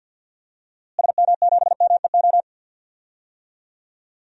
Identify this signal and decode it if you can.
Morse code, heard as a keyed tone, about 51 wpm, 700 Hz. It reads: SM7MEO
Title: DXCC-100 Call Signs - 6 Characters - Repeated 3x - 50wpm